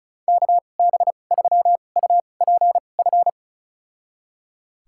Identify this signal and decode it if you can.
Morse code, heard as a keyed tone, about 36 wpm, spaced as for 25 wpm, 690 Hz. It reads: KB3UPF
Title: USA Call Signs - 6 Characters - 2x Character Spacing - 35wpm